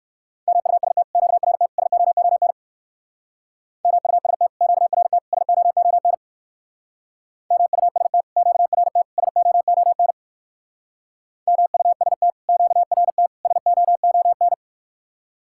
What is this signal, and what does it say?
Morse code, heard as a keyed tone, about 53 wpm, 700 Hz. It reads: MUST QRT SOON MUST QRT SOON MUST QRT SOON MUST QRT SOON
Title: Speed-Racing Common QSO Phrases - 45wpm